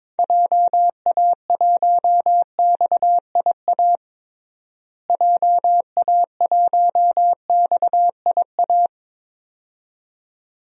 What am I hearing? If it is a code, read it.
Morse code, heard as a keyed tone, about 22 wpm, 700 Hz. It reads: JA1XIA JA1XIA
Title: DXCC-100 Call Signs - 6 Characters - Repeated 2x - 22wpm